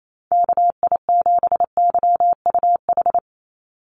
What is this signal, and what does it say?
Morse code, heard as a keyed tone, about 28 wpm, 700 Hz. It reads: KI7YUH
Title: USA Call Signs 28wpm